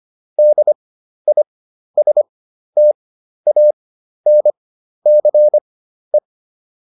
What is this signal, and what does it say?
Morse code, heard as a keyed tone, about 25 wpm, spaced as for 12 wpm, 600 Hz. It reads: DISTANCE